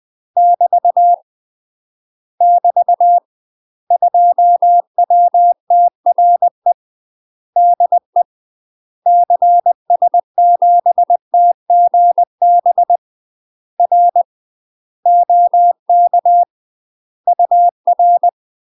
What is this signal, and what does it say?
Morse code, heard as a keyed tone, about 20 wpm, 700 Hz. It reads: <BT> <BT> 2WTRE DE CS7TGB R OK UR
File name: UK MASTER QSOs TRAINING / 20WPM / UK MASTER QSOs 0047